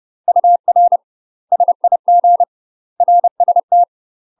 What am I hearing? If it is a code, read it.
Morse code, heard as a keyed tone, about 30 wpm, 700 Hz. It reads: UR SIG RST
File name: UK MASTER QSOs TRAINING/30WPM